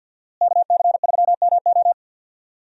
Morse code, heard as a keyed tone, about 49 wpm, 700 Hz. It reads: KC3MO
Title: USA Call Signs - 5 Characters - Repeated 3x - 50wpm